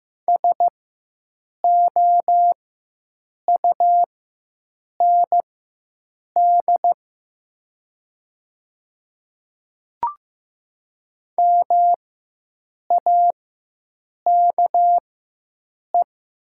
Morse code, heard as a keyed tone, about 15 wpm, spaced as for 7 wpm, 700 Hz. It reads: SOUND MAKE
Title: 4x Character Spacing - Top 200 Words - 15wpm